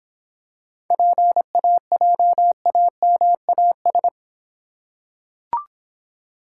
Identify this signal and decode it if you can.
Morse code, heard as a keyed tone, about 26 wpm, 700 Hz. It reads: PAJAMAS